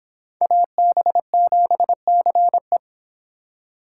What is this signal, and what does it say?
Morse code, heard as a keyed tone, about 25 wpm, 700 Hz. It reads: AB7CE